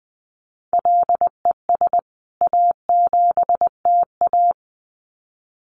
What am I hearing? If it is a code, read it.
Morse code, heard as a keyed tone, about 20 wpm, 700 Hz. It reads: LES A7TA
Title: CWT Exchange 20wpm